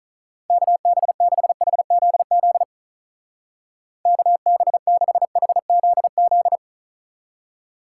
Morse code, heard as a keyed tone, about 44 wpm, 700 Hz. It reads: KB6HZZ KB6HZZ